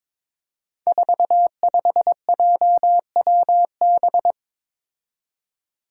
Morse code, heard as a keyed tone, about 22 wpm, 700 Hz. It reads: 45JWB
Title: Five Challenging Letter-Number Combinations 22wpm